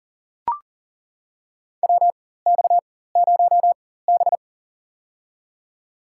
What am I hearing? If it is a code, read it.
Morse code, heard as a keyed tone, about 40 wpm, spaced as for 19 wpm, 700 Hz. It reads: WX0B
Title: USA Call Signs - 4 Characters - 4x Character Spacing - 40wpm